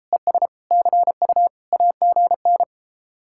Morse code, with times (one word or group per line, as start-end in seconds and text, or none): 0.1-0.5 ES
0.7-1.5 CU
1.7-2.6 AGN